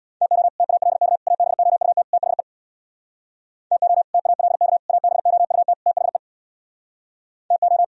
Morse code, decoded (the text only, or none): TO TALK ABOUT THE TO TALK ABOUT THE TO